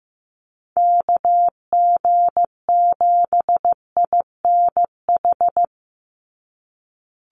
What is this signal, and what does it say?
Morse code, heard as a keyed tone, about 14 wpm, 700 Hz. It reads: KG7INH
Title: USA Call Signs 15wpm